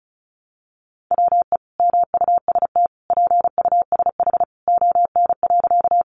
Morse code, heard as a keyed tone, about 34 wpm, 700 Hz. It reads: WE MUST PUSH ON.
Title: Sentences from Top 1300 Words 35wpm